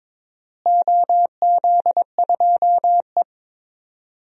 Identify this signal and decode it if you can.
Morse code, heard as a keyed tone, about 22 wpm, 700 Hz. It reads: OZ2E
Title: DXCC-100 Call Signs - 4 Characters - Repeated 2x - 22wpm